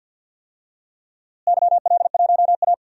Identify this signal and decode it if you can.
Morse code, heard as a keyed tone, about 52 wpm, 700 Hz. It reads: YL1A
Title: DXCC-100 Call Signs - 4 Characters - Repeated 2x - 50wpm